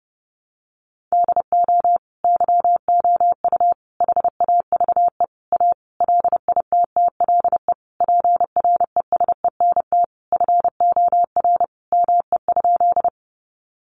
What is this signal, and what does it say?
Morse code, heard as a keyed tone, about 30 wpm, 700 Hz. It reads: DO YOU HAVE A LITTLE PRESENT FOR ME?
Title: Sentences from Top 800 Words 30wpm